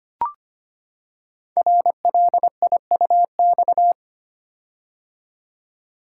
Morse code, heard as a keyed tone, about 25 wpm, 690 Hz. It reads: RLIUX
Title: Code Groups - 5 Letter:Numbers - 25wpm